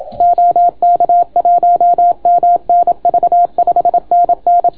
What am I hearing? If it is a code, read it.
Morse code, heard as a keyed tone, about 31 wpm, 690 Hz. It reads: OK1MNV 5NN